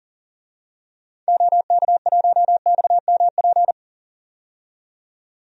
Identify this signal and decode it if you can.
Morse code, heard as a keyed tone, about 38 wpm, 700 Hz. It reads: OK1XMP
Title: DXCC-100 Call Signs - 6 Characters - 40wpm